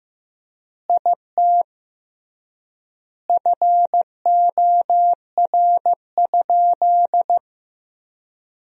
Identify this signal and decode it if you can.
Morse code, heard as a keyed tone, about 15 wpm, 700 Hz. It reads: IT FOR?
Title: 3x Word-Spacing - Sentences from Top 100 Words - 15wpm